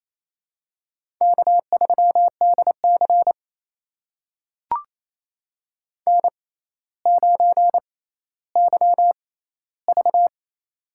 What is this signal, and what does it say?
Morse code, heard as a keyed tone, about 28 wpm, spaced as for 14 wpm, 700 Hz. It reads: K3DC N9YV
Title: USA Call Signs - 4 Characters - 6x Character Spacing - 28wpm